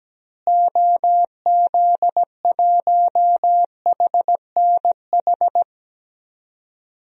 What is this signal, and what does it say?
Morse code, heard as a keyed tone, about 17 wpm, 700 Hz. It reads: OZ1HNH
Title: DXCC-100 Call Signs - 6 Characters - Repeated 3x - 17wpm